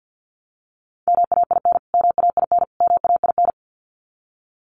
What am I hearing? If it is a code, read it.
Morse code, heard as a keyed tone, about 49 wpm, 700 Hz. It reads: MAIN MAIN MAIN
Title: Top 300 Words Repeated 50wpm